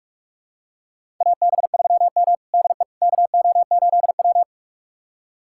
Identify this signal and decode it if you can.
Morse code, heard as a keyed tone, about 42 wpm, 700 Hz. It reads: AB3M DE KO8W
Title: USA Callsign DE Callsign - 4 Characters - 45wpm